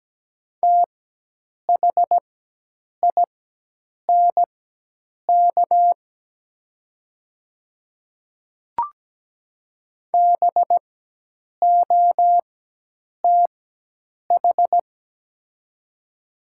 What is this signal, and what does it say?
Morse code, heard as a keyed tone, about 17 wpm, spaced as for 8 wpm, 700 Hz. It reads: THINK BOTH
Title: 4x Character Spacing - Top 300 Words - 17wpm